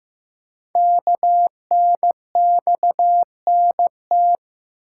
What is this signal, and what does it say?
Morse code, heard as a keyed tone, about 15 wpm, 700 Hz. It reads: KNXNT
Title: Code Groups - 5 Letters - 15wpm